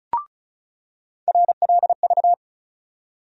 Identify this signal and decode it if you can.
Morse code, heard as a keyed tone, about 36 wpm, 700 Hz. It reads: RLV
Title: USA Call Sign Postfixes - 3 Letters 35wpm